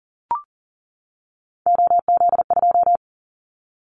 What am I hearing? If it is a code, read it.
Morse code, heard as a keyed tone, about 38 wpm, 700 Hz. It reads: OZ2
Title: Callsign Prefixes for Top 50 DXCC - 40wpm